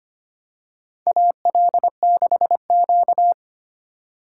Morse code, heard as a keyed tone, about 25 wpm, 700 Hz. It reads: AL6Q
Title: USA Call Signs - 4 Characters - 4x Character Spacing - 25wpm